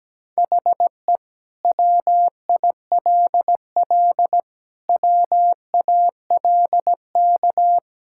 Morse code, heard as a keyed tone, about 17 wpm, 700 Hz. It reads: HE WILL WALK
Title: Sentences from Top 1600 Words 17wpm